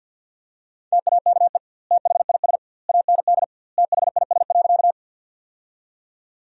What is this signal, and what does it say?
Morse code, heard as a keyed tone, about 50 wpm, 700 Hz. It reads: TAKE THIS AND THIS.
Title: 2x Word-Spacing - Sentences from Top 200 Words - 50wpm